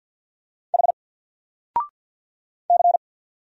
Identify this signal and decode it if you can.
Morse code, heard as a keyed tone, about 50 wpm, 700 Hz. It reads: H C
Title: Single Letter-Number Alt 50wpm